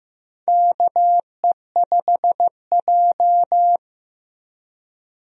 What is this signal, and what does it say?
Morse code, heard as a keyed tone, about 16 wpm, 700 Hz. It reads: KE5J
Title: USA Call Signs - 4 Characters - 15wpm